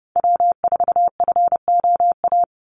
Morse code, heard as a keyed tone, about 29 wpm, 700 Hz. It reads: W4FOA
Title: Call Signs - 30wpm